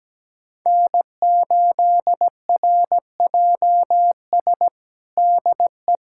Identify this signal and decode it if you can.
Morse code, heard as a keyed tone, about 17 wpm, 700 Hz. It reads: N8RJS DE
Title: USA Callsign DE Callsign 17wpm